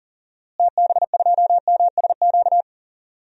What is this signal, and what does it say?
Morse code, heard as a keyed tone, about 40 wpm, 700 Hz. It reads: TB2MSQ
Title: DXCC-100 Call Signs - 6 Characters - Repeated 3x - 40wpm